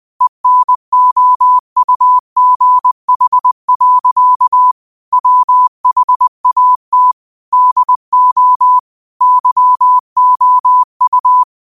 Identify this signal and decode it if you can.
Morse code, heard as a keyed tone, about 20 wpm, 1000 Hz. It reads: ENOUGH. WHAT DO YOU